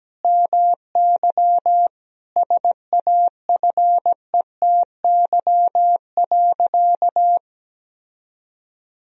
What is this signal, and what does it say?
Morse code, heard as a keyed tone, about 17 wpm, 700 Hz. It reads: MY SAFETY.
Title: Sentences from Top 1000 Words - Review Part 4 - 17wpm